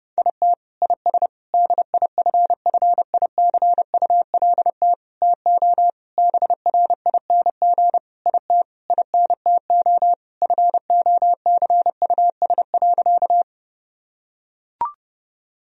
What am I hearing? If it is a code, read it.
Morse code, heard as a keyed tone, about 30 wpm, 700 Hz. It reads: IT IS DIFFICULT TO BRING IT INTO FOCUS.
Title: Sentences from Top 1000 Words - Review Part 4 - 30wpm